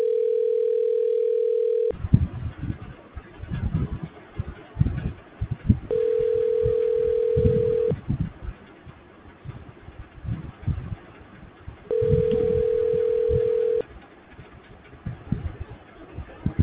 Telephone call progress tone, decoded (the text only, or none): ringback tone